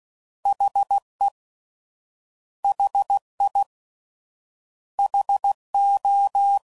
Morse code, heard as a keyed tone, about 16 wpm, 800 Hz. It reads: HE HI HO